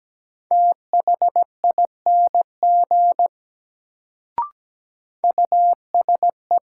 Morse code, heard as a keyed tone, about 17 wpm, 700 Hz. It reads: THING USE
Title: Sentences from Top 100 Words - New Word Drill - 17wpm